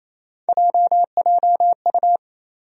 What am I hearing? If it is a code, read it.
Morse code, heard as a keyed tone, about 28 wpm, 700 Hz. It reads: JJU
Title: USA Call Sign Postfixes - 3 Letters 28wpm